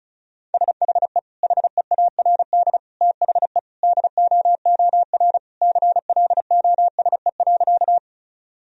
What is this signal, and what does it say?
Morse code, heard as a keyed tone, about 36 wpm, 700 Hz. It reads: SHE HEARD THE DOOR CLOSE.
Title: Sentences from Top 1000 Words - Review Part 1 - 35wpm